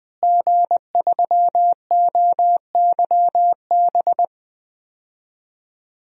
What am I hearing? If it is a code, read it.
Morse code, heard as a keyed tone, about 21 wpm, 700 Hz. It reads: G3OYB
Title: DXCC-100 Call Signs - 5 Characters - Repeated 2x - 20wpm